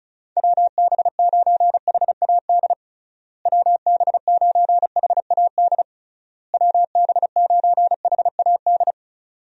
Morse code, heard as a keyed tone, about 34 wpm, 700 Hz. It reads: WB9HAD WB9HAD WB9HAD
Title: USA Call Signs - 6 Characters - Repeated 3x - 35wpm